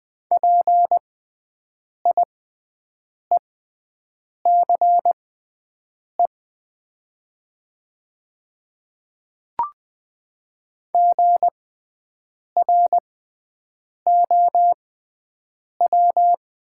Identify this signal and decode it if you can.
Morse code, heard as a keyed tone, about 20 wpm, spaced as for 7 wpm, 700 Hz. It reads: PIECE GROW
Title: 6x Character Spacing - Top 500 Words - 20wpm